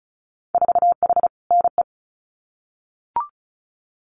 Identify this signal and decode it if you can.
Morse code, heard as a keyed tone, about 36 wpm, 700 Hz. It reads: 4H NE
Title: Winter Field Day Exchanges Encore 35wpm